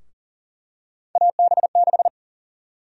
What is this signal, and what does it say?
Morse code, heard as a keyed tone, about 40 wpm, 700 Hz. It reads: AB6
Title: USA Call Sign Prefixes - 40wpm